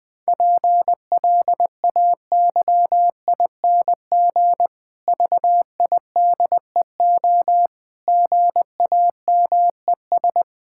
Morse code, heard as a keyed tone, about 20 wpm, 700 Hz. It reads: PLAYING VIDEO GAMES
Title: Difficult Sentences with Hints - 20wpm